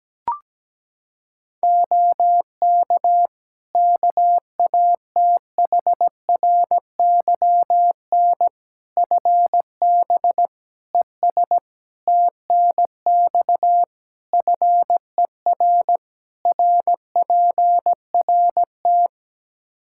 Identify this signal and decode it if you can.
Morse code, heard as a keyed tone, about 17 wpm, 700 Hz. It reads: OK KATHRYN FB ES TNX FER RPRT
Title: K1USN SST QSO Practice 17wpm